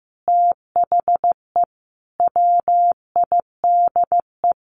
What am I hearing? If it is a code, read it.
Morse code, heard as a keyed tone, about 14 wpm, 700 Hz. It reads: THE WIDE
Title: Sentences from Top 900 Words 15wpm